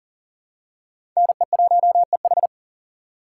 Morse code, heard as a keyed tone, about 40 wpm, 700 Hz. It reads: NE1EH
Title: USA Call Signs - 5 Characters - 6x Character Spacing - 40wpm